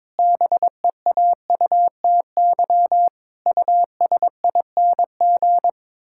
Morse code, heard as a keyed tone, about 22 wpm, 700 Hz. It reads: BEAUTY USING
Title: Pangram Challenge v2 - 22wpm